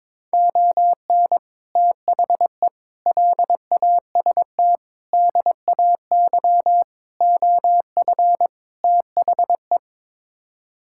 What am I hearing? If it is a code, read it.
Morse code, heard as a keyed tone, about 22 wpm, 700 Hz. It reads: ON THE LAST DAY OF THE
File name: practice / 22wpm / Sets of 6 Words - 22wpm